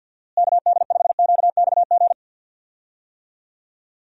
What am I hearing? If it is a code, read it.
Morse code, heard as a keyed tone, about 49 wpm, 700 Hz. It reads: KD5QXG
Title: USA Call Signs - 6 Characters - Repeated 3x - 50wpm